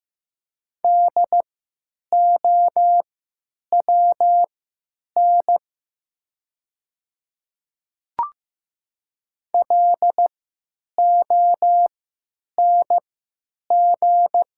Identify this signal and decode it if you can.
Morse code, heard as a keyed tone, about 15 wpm, spaced as for 9 wpm, 700 Hz. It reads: DOWN LONG
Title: 3x Character Spacing - Top 100 Words - 15wpm